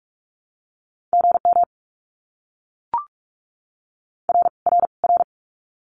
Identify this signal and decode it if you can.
Morse code, heard as a keyed tone, about 44 wpm, 700 Hz. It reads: GM R R R
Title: Abbreviations Repeated 3x - 45wpm